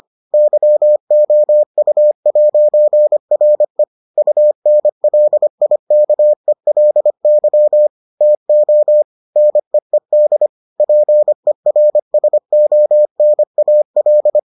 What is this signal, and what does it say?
Morse code, heard as a keyed tone, about 25 wpm, 600 Hz. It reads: YOU'RE UNLIKELY TO NEED PERSONAL